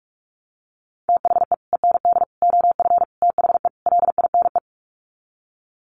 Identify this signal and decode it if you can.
Morse code, heard as a keyed tone, about 44 wpm, 700 Hz. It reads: THE END OF THE LINE
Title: Sets of 5 Words Encore2 45wpm